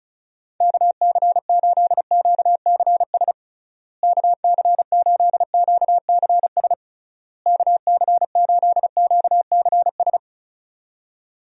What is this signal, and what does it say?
Morse code, heard as a keyed tone, about 34 wpm, 700 Hz. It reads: KC8QCS KC8QCS KC8QCS
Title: USA Call Signs - 6 Characters - Repeated 3x - 35wpm